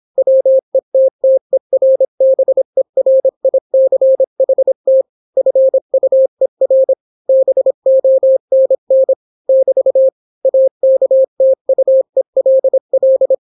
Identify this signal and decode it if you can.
Morse code, heard as a keyed tone, about 26 wpm, 530 Hz. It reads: WETTERBERICHT FUER BONN <BT> AKTUELL